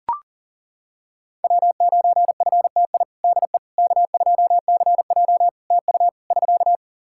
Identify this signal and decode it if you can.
Morse code, heard as a keyed tone, about 40 wpm, 700 Hz. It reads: W9FTI DE K2CJ TU <SK>
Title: K1USN SST QSO Practice 40wpm